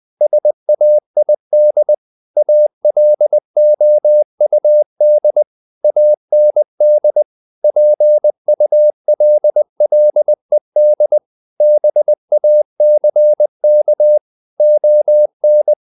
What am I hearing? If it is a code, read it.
Morse code, heard as a keyed tone, about 20 wpm, 600 Hz. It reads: SAID ALOUD AND PULLED BACK ON
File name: the old man and the sea / the old man and the sea part 1 20wpm